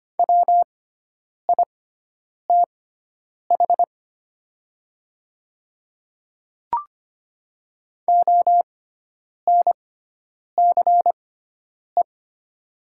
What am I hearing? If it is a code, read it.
Morse code, heard as a keyed tone, about 25 wpm, spaced as for 9 wpm, 700 Hz. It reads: WITH ONCE
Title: 6x Character Spacing - Top 300 Words - 25wpm